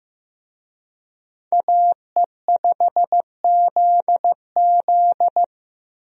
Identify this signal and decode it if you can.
Morse code, heard as a keyed tone, about 15 wpm, 700 Hz. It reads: AE5ZZ
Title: USA Call Signs - 5 Characters - Repeated 2x - 15wpm